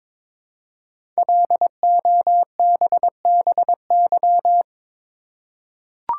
Morse code, heard as a keyed tone, about 22 wpm, 700 Hz. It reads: LOBBY
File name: 5 Letter Words - First 3 Letters Given - 22wpm